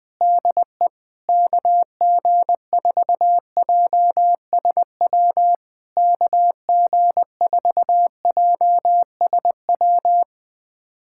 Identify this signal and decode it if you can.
Morse code, heard as a keyed tone, about 21 wpm, 700 Hz. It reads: DE KG4JSW KG4JSW
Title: SOTA QSO Practice 20wpm